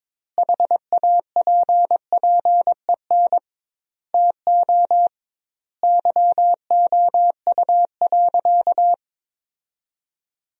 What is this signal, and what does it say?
Morse code, heard as a keyed tone, about 22 wpm, 700 Hz. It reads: HAPPEN TO YOU.
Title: 2x Word-Spacing - Sentences from Top 500 Words - 22wpm